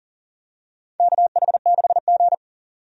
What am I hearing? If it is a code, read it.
Morse code, heard as a keyed tone, about 40 wpm, 700 Hz. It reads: KH6G
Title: DXCC-100 Call Signs - 4 Characters - 40wpm